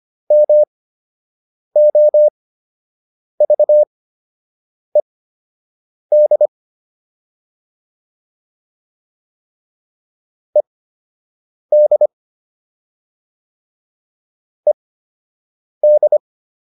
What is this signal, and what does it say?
Morse code, heard as a keyed tone, about 25 wpm, spaced as for 7 wpm, 600 Hz. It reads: MOVED ED ED